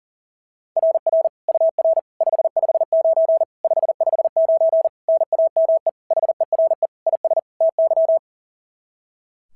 Morse code, heard as a keyed tone, about 37 wpm, 660 Hz. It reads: RR UR 559 559 NAME HERE IS TY
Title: SOTA QSO Practice 40wpm